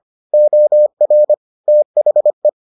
Morse code, heard as a keyed tone, about 25 wpm, 600 Hz. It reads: OR THE